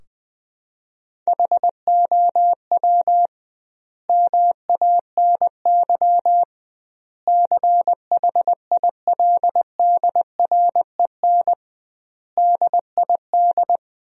Morse code, heard as a keyed tone, about 21 wpm, 700 Hz. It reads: HOW MANY CHILDREN DID
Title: 2x Word-Spacing - Sentences from Top 400 Words - 20wpm